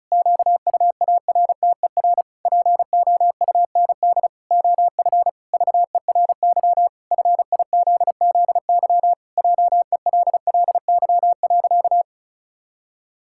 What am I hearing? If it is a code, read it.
Morse code, heard as a keyed tone, about 36 wpm, 700 Hz. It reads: QUARTER POUND OF VERY FIZZY JELLY.
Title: Pangram Challenge v2 - 35wpm